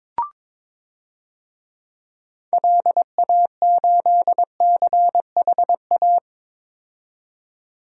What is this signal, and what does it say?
Morse code, heard as a keyed tone, about 22 wpm, 700 Hz. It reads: LA8CHA
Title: DXCC-100 Call Signs - 6 Characters - 22wpm